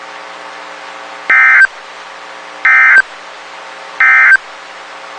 Emergency Alert System (SAME) tones, heard as harsh buzzing bursts. It is heard from 1.2 to 4.5 s.
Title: WXK91 NOAA Weather Radio